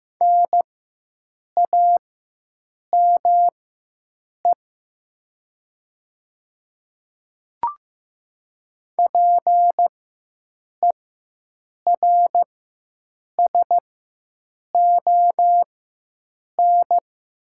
Morse code, heard as a keyed tone, about 15 wpm, spaced as for 7 wpm, 700 Hz. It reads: NAME PERSON
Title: 4x Character Spacing - Top 500 Words - 15wpm